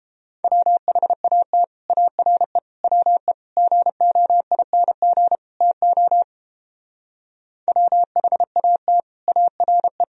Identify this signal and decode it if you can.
Morse code, heard as a keyed tone, about 29 wpm, 700 Hz. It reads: WHAT ARE WE GOING TO WHAT ARE